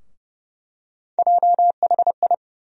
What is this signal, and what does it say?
Morse code, heard as a keyed tone, about 30 wpm, 700 Hz. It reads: JHI
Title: USA Call Sign Postfixes - 3 Letters 30wpm